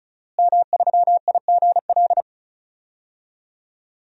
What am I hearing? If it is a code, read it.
Morse code, heard as a keyed tone, about 36 wpm, 700 Hz. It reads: M3IGL
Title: DXCC-100 Call Signs - 5 Characters - Repeated 2x - 35wpm